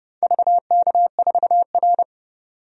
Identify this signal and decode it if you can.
Morse code, heard as a keyed tone, about 30 wpm, 700 Hz. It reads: VK4R
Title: DXCC-100 Call Signs - 4 Characters - 30wpm